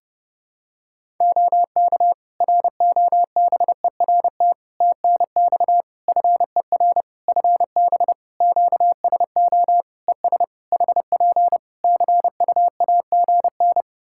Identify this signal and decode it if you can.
Morse code, heard as a keyed tone, about 30 wpm, 700 Hz. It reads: OK ROBERT TNX FER FB QSO ES HP CUAGN
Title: Practice QSO - Exchange 4 - Ending - 30wpm